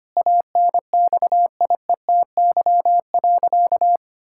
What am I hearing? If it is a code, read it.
Morse code, heard as a keyed tone, about 25 wpm, 700 Hz. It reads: ANXIETY.
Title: Sentences from Top 1600 Words 25wpm